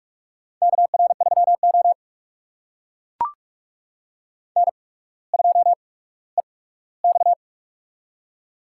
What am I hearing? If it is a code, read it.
Morse code, heard as a keyed tone, about 44 wpm, spaced as for 19 wpm, 690 Hz. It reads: KR3O N2EX
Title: USA Call Signs - 4 Characters - 8x Character Spacing - 45wpm